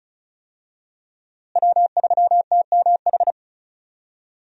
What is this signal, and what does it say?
Morse code, heard as a keyed tone, about 36 wpm, 700 Hz. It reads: W3TMH
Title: USA Call Signs - 5 Characters - Repeated 2x - 35wpm